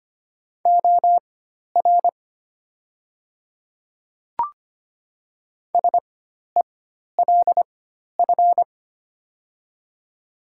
Morse code, heard as a keyed tone, about 25 wpm, spaced as for 12 wpm, 700 Hz. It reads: OR SELF
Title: 4x Character Spacing - Top 500 Words - 25wpm